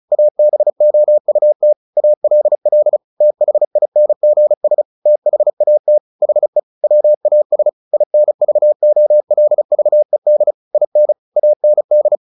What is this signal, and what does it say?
Morse code, heard as a keyed tone, about 34 wpm, 600 Hz. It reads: ABOUT ALL THINGS THAT HE WAS INVOLVED IN AND
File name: the old man and the sea 2 35wpm